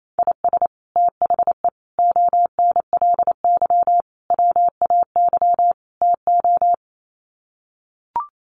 Morse code, heard as a keyed tone, about 28 wpm, 700 Hz. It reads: IS THE ONLY WAY TO
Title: Sets of 5 Words Encore 28wpm